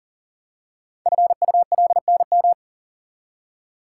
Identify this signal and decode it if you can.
Morse code, heard as a keyed tone, about 41 wpm, 700 Hz. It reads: FULNM
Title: Code Groups - 5 Letters - 40wpm